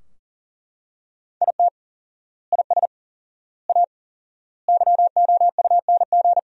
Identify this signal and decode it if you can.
Morse code, heard as a keyed tone, about 40 wpm, 700 Hz. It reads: IT IS A YOUNG
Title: 4x Word-Spacing - Sentences from Top 400 Words - 40wpm